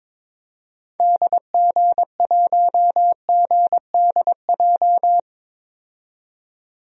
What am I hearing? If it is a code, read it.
Morse code, heard as a keyed tone, about 22 wpm, 700 Hz. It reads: DG1GDJ
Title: DXCC-100 Call Signs - 6 Characters - 22wpm